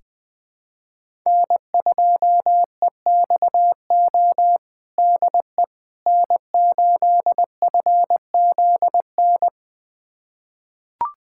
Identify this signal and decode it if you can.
Morse code, heard as a keyed tone, about 20 wpm, 700 Hz. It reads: N2EXO DE N8FZN
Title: USA Callsign DE Callsign - 5 Characters - 20wpm